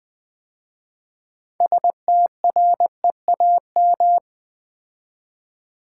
Morse code, heard as a keyed tone, about 20 wpm, 700 Hz. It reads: STREAM
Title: 6 Letter Words - First 3 Letters Given - 20wpm